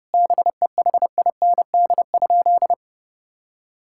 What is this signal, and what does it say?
Morse code, heard as a keyed tone, about 30 wpm, 700 Hz. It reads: BEHIND?
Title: Sentences from Top 1000 Words - Review Part 5 - 30wpm